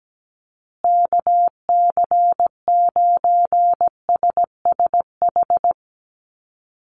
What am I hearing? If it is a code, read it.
Morse code, heard as a keyed tone, about 17 wpm, 700 Hz. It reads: KC9SSH
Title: USA Call Signs Repeated 2x 17wpm